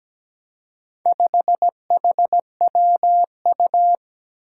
Morse code, heard as a keyed tone, about 17 wpm, 700 Hz. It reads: 5HWU
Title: Four Challenging Letter-Number Combinations 17wpm